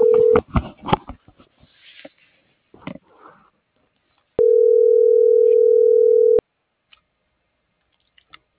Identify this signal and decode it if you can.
Ringback tone (North America), 440 + 480 Hz.